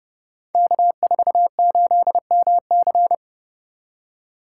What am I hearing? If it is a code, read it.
Morse code, heard as a keyed tone, about 30 wpm, 700 Hz. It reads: K48MC
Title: Code Groups - 5 Letter:Numbers - 30wpm